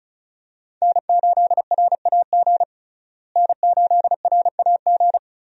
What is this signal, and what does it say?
Morse code, heard as a keyed tone, about 34 wpm, 700 Hz. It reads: N8RAG N8RAG
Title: USA Call Signs - 5 Characters - Repeated 2x - 35wpm